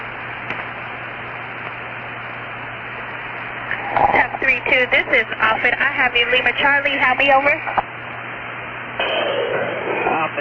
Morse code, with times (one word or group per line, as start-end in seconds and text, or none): none